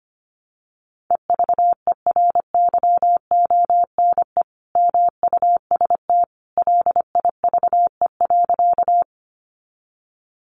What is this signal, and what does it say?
Morse code, heard as a keyed tone, about 25 wpm, 700 Hz. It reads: EVERYONE MUST LIVE.
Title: Sentences from Top 800 Words 25wpm